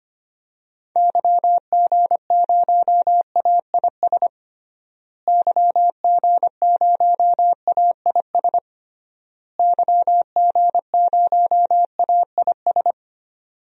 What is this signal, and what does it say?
Morse code, heard as a keyed tone, about 25 wpm, 700 Hz. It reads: YG0AIS YG0AIS YG0AIS
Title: DXCC-100 Call Signs - 6 Characters - Repeated 3x - 25wpm